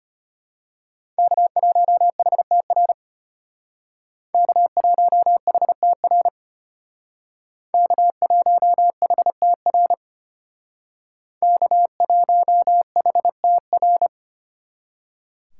Morse code, heard as a keyed tone, about 32 wpm, 700 Hz. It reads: K1HTR K1HTR K1HTR K1HTR